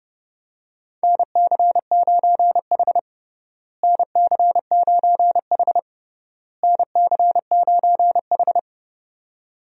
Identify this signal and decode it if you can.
Morse code, heard as a keyed tone, about 32 wpm, 700 Hz. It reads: NC9H NC9H NC9H
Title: USA Call Signs - 4 Characters - Repeated 3x - 30wpm